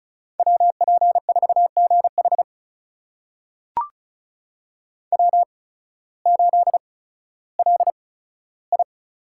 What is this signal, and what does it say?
Morse code, heard as a keyed tone, about 36 wpm, spaced as for 16 wpm, 700 Hz. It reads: WP4GH W8LI
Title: USA Call Signs - 8x Character Spacing - 35wpm